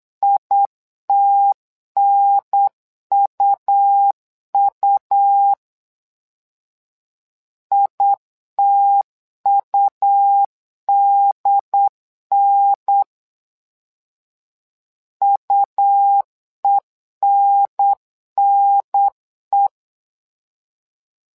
Morse code, heard as a keyed tone, about 8 wpm, 800 Hz. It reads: ITNUU ITUDN UENNE
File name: LEZIONI / 2 UHDN / 2 UHDN 8wpm2232